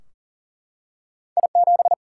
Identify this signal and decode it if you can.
Morse code, heard as a keyed tone, about 38 wpm, 700 Hz. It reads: I7
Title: Single-Letter Single-Number 40wpm